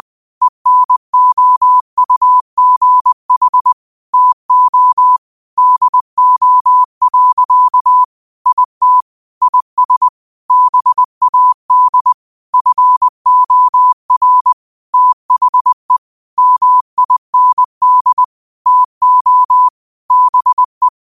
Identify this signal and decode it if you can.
Morse code, heard as a keyed tone, about 20 wpm, 1000 Hz. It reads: ENOUGH TO DO. IT IS BAD FOR THE MIND TO BE
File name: qotdfiles / 20wpm